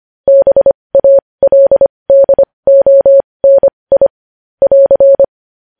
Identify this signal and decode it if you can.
Morse code, heard as a keyed tone, about 25 wpm, spaced as for 20 wpm, 550 Hz. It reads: BALDONI <AR>
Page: news-headlines-in-morse-code-at-20-wpm